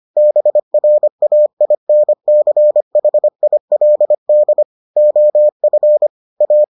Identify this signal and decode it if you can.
Morse code, heard as a keyed tone, about 25 wpm, 600 Hz. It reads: BRAINCHILD OF A